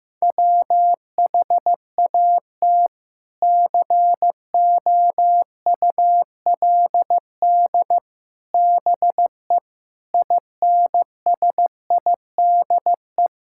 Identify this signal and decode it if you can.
Morse code, heard as a keyed tone, about 16 wpm, 700 Hz. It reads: WHAT COULD BE INSIDE
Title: Sentences from Top 1000 Words - Review Part 6 - 15wpm